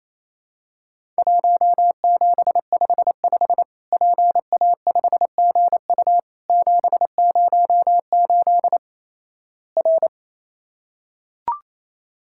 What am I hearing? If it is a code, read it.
Morse code, heard as a keyed tone, about 28 wpm, 700 Hz. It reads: 1755 PA5GU 708 R
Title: Worked All Europe DX Contest - QTC Exchange - 28wpm